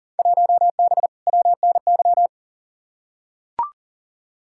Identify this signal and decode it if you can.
Morse code, heard as a keyed tone, about 40 wpm, 700 Hz. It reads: 1B WNY
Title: Field Day Exchanges - 40wpm